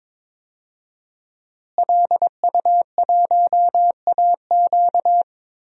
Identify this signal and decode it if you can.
Morse code, heard as a keyed tone, about 22 wpm, 700 Hz. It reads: LU1AQ
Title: DXCC-100 Call Signs - 5 Characters - Repeated 2x - 22wpm